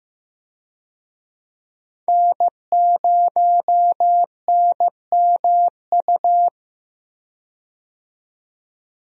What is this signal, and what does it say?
Morse code, heard as a keyed tone, about 15 wpm, 700 Hz. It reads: N0NMU
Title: USA Call Signs - 5 Characters - 15wpm